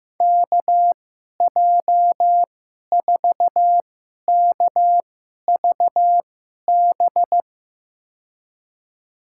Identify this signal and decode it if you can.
Morse code, heard as a keyed tone, about 15 wpm, spaced as for 11 wpm, 700 Hz. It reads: KJ4KVB
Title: USA Call Signs - 6 Characters - 2x Character Spacing - 15wpm